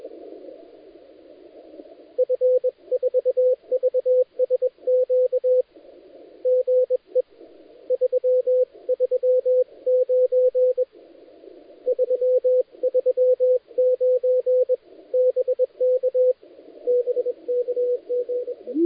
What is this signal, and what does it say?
Morse code, heard as a keyed tone, about 21 wpm, 510 Hz, with radio noise. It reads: F4VSQ GE 339 339 BK BK GE
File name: dlazaawansowanych / nagraniazpasma / titawaNETcontrol